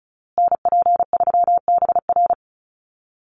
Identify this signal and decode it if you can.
Morse code, heard as a keyed tone, about 34 wpm, 700 Hz. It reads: NP3BR
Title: USA Call Signs Repeated 2x 35wpm